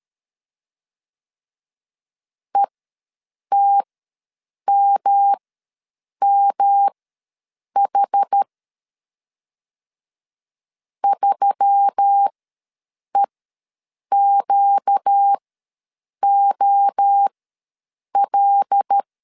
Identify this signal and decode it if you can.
Morse code, heard as a keyed tone, about 13 wpm, spaced as for 7 wpm, 800 Hz. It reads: ETMMH 3EQOL
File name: LEZIONI / 5 LBVC / 5 L ws 12wpm 2958